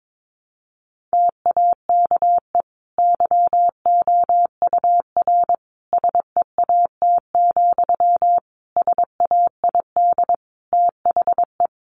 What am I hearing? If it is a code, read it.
Morse code, heard as a keyed tone, about 22 wpm, 700 Hz. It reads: TAKE YOUR SEAT, SAID THE
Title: Sentences from Top 1500 Words 22wpm